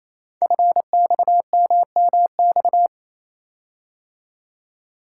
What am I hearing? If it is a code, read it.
Morse code, heard as a keyed tone, about 29 wpm, 700 Hz. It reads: FXMMX
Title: Code Groups - 5 Letters - 28wpm